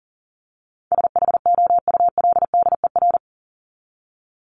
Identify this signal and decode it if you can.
Morse code, heard as a keyed tone, about 40 wpm, 700 Hz. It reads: SHOULDER
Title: Top 1501-1750 Words 40wpm